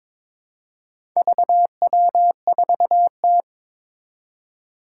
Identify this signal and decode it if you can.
Morse code, heard as a keyed tone, about 22 wpm, 700 Hz. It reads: VW4T
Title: Four Challenging Letter-Number Combinations 22wpm